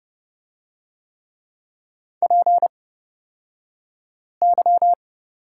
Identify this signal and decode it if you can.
Morse code, heard as a keyed tone, about 30 wpm, spaced as for 5 wpm, 690 Hz. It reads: PY
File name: Lesson 36d - BT - Two Characters 30wpm